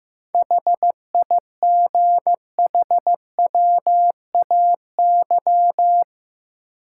Morse code, heard as a keyed tone, about 15 wpm, 700 Hz. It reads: HIGHWAY
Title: Sentences from Top 1400 Words - New Word Drill - 15wpm